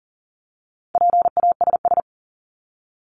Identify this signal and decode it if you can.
Morse code, heard as a keyed tone, about 40 wpm, 700 Hz. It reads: PASS
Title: ING Suffix - Top 100 Words with Base Words 40wpm